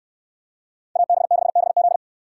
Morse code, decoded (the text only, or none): AB6BZ